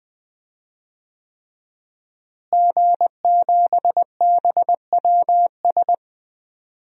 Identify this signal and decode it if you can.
Morse code, heard as a keyed tone, about 20 wpm, 700 Hz. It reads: G7BWS
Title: DXCC-100 Call Signs - 5 Characters - Repeated 3x - 20wpm